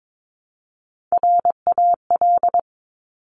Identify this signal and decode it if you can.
Morse code, heard as a keyed tone, about 22 wpm, 700 Hz. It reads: RAL
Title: Top Three Letter Combinations in Words 22wpm